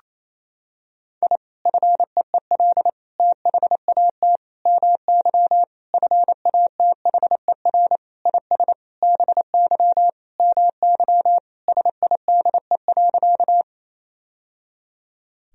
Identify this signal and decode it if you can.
Morse code, heard as a keyed tone, about 28 wpm, 700 Hz. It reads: I FEEL THAT MY FATHER IS BY MY SIDE.
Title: Sentences from Top 1000 Words - Review Part 1 - 28wpm